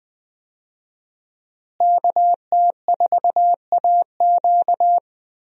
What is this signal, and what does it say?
Morse code, heard as a keyed tone, about 20 wpm, 700 Hz. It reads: KT4AQ
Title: USA Call Signs - 5 Characters - Repeated 3x - 20wpm